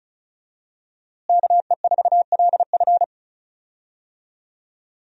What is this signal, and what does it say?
Morse code, heard as a keyed tone, about 36 wpm, 700 Hz. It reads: KE4LF
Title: USA Call Signs - 5 Characters - Repeated 2x - 35wpm